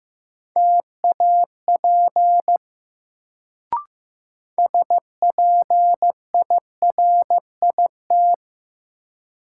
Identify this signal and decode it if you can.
Morse code, heard as a keyed tone, about 15 wpm, 700 Hz. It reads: TAP SPIRIT
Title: Sentences from Top 1400 Words - New Word Drill Alt - 15wpm